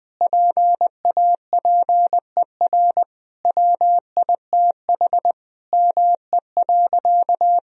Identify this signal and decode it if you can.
Morse code, heard as a keyed tone, about 20 wpm, 700 Hz. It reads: PAPER WITH ME.
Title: Sentences from Top 1000 Words - Review Part 2 - 20wpm